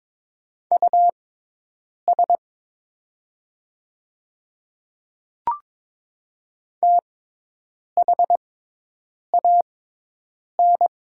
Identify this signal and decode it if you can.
Morse code, heard as a keyed tone, about 22 wpm, spaced as for 8 wpm, 690 Hz. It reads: US THAN
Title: 6x Character Spacing - Top 500 Words - 22wpm